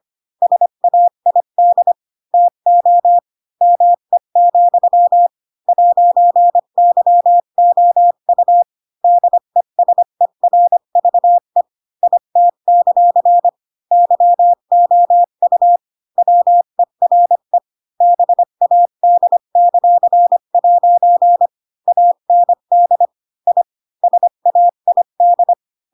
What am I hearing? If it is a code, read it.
Morse code, heard as a keyed tone, about 25 wpm, 700 Hz. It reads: SAID TO ME, 'YOU DESERVE IT; YOU WERE BAD;' AND I SAID
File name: Pinocchio / Pinocchio ch7 20 25 ik1hge